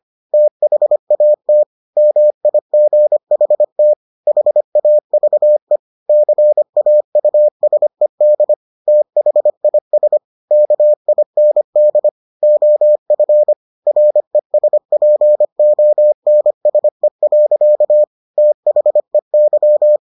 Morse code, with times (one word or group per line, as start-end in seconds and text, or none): 0.3-1.6 THAT
2.0-3.9 MIGHT
4.3-5.8 HAVE
6.1-8.5 CAUSED
8.9-10.2 THIS
10.5-12.1 KIND
12.4-13.5 OF
13.9-18.0 RESPONSE.
18.4-20.0 THEY